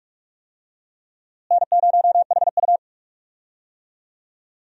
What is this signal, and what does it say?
Morse code, heard as a keyed tone, about 46 wpm, 700 Hz. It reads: N0HU
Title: USA Call Signs - 4 Characters - Repeated 2x - 45wpm